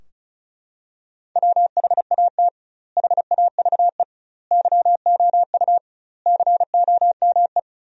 Morse code, heard as a keyed tone, about 36 wpm, 700 Hz. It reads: WHAT HAVE YOU COME
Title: 2x Word-Spacing - Sentences from Top 100 Words - 35wpm